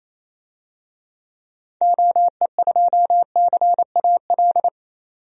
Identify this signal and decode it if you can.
Morse code, heard as a keyed tone, about 28 wpm, 700 Hz. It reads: OE2CAL
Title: DXCC-100 Call Signs - 6 Characters - 28wpm